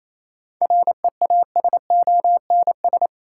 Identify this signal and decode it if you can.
Morse code, heard as a keyed tone, about 27 wpm, 700 Hz. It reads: REASONS
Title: Sets of 4 Words - New Word Drill - 28wpm